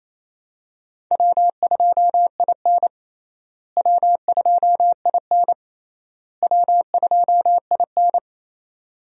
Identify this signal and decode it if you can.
Morse code, heard as a keyed tone, about 27 wpm, 700 Hz. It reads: W2IN W2IN W2IN
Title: USA Call Signs - 4 Characters - Repeated 3x - 28wpm